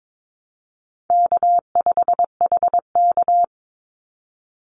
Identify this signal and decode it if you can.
Morse code, heard as a keyed tone, about 22 wpm, 700 Hz. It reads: K5HK
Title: Easy Call Signs Repeated 2x - 22wpm